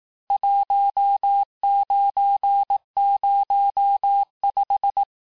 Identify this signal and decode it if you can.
Morse code, heard as a keyed tone, about 18 wpm, 800 Hz. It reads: 1905
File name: cours-du-lundi files / cours-1852-texte-clair-vitesse-17MPM-puis-22MPM